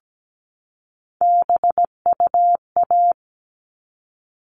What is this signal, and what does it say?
Morse code, heard as a keyed tone, about 17 wpm, 700 Hz. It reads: BUA
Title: All Cut Numbers - 3-Digit 17wpm